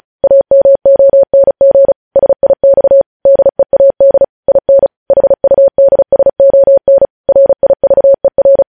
Morse code, heard as a keyed tone, about 36 wpm, 550 Hz. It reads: AMONG SIX DEAD IN HUDSON RIVER
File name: podcast / News headlines-1x-35x35